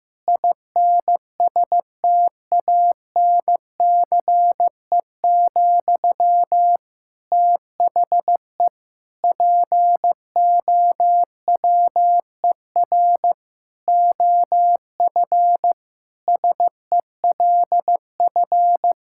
Morse code, heard as a keyed tone, about 15 wpm, 700 Hz. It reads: INSTANCE, THE POWER OF SELF
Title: Sentences from Top 1000 Words - Review Part 5 - 15wpm